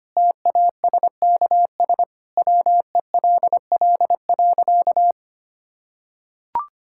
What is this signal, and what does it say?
Morse code, heard as a keyed tone, about 25 wpm, 700 Hz. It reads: TASKS WELL.
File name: Easy Sentences with Hints - 25wpm